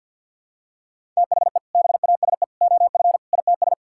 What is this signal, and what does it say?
Morse code, heard as a keyed tone, about 50 wpm, 700 Hz. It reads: THE BASE OF ITS
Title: Sentences from Top 1000 Words - Review Part 5 - 50wpm